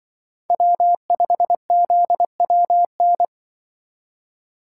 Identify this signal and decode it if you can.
Morse code, heard as a keyed tone, about 24 wpm, 700 Hz. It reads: W5ZWN
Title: Speed Racing - USA 1x3 Call Signs 20wpm